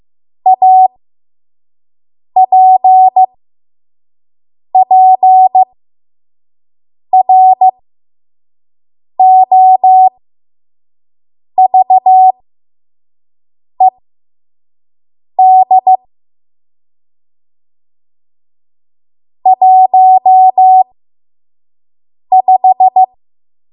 Morse code, heard as a keyed tone, about 15 wpm, spaced as for 5 wpm, 750 Hz. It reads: APPROVED 15
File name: Morse / 05WPM